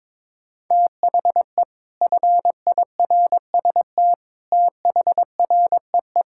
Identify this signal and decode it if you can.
Morse code, heard as a keyed tone, about 22 wpm, 700 Hz. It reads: THE FIRST THREE
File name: Sentences with 7-Word Phrases - 22wpm